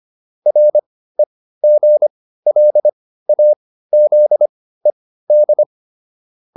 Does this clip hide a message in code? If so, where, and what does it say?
Yes: Morse code from 0.5 to 5.6 s, heard as a keyed tone, about 25 wpm, spaced as for 15 wpm, 600 Hz. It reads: REGLAZED